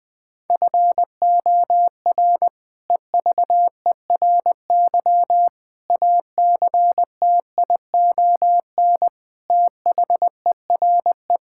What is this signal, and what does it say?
Morse code, heard as a keyed tone, about 21 wpm, 700 Hz. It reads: FOR EVERY ACTION THERE
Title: Sets of 8 Words - 20wpm